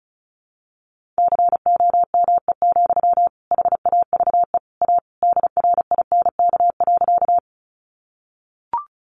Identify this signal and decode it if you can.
Morse code, heard as a keyed tone, about 34 wpm, 700 Hz. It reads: COME, HAVE A DRINK.
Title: Sentences from Top 1500 Words 35wpm